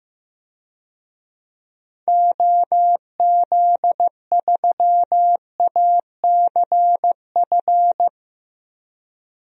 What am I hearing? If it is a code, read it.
Morse code, heard as a keyed tone, about 15 wpm, 700 Hz. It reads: OZ3ACF
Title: DXCC-100 Call Signs - 6 Characters - 15wpm